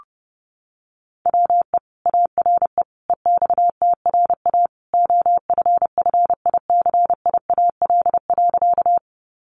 Morse code, heard as a keyed tone, about 30 wpm, 700 Hz. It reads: WE ARE EXTRA OFFICIAL.
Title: Sentences from Top 1000 Words 30wpm